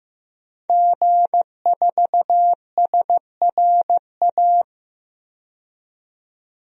Morse code, heard as a keyed tone, about 15 wpm, 700 Hz. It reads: G4SRA
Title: DXCC-100 Call Signs - 5 Characters - 15wpm